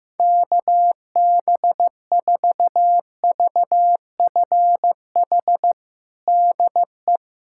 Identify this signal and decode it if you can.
Morse code, heard as a keyed tone, about 16 wpm, 700 Hz. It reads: KB4VFH DE
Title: USA Callsign DE Callsign 15wpm